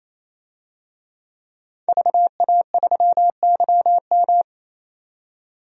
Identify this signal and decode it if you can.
Morse code, heard as a keyed tone, about 28 wpm, 700 Hz. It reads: VA3YM
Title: DXCC-100 Call Signs - 5 Characters - 28wpm